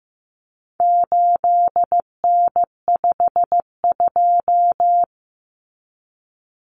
Morse code, heard as a keyed tone, about 15 wpm, 700 Hz. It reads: 8N52
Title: Common Cut Numbers - 4-Digit 15wpm